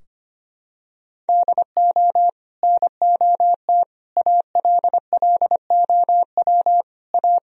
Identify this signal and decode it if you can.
Morse code, heard as a keyed tone, about 25 wpm, 700 Hz. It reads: DO NOT ALLOW A
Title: Sentences from Top 1000 Words - Review Part 4 - 25wpm